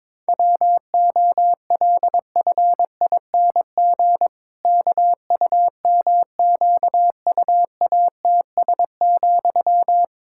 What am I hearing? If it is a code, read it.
Morse code, heard as a keyed tone, about 21 wpm, 700 Hz. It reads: WOLFING KUMQUATS,
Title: Pangram Challenge v2 - 22wpm